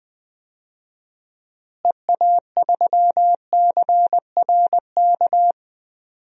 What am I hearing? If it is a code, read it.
Morse code, heard as a keyed tone, about 20 wpm, 700 Hz. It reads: EA3CRK
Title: DXCC-100 Call Signs - 6 Characters - Repeated 2x - 20wpm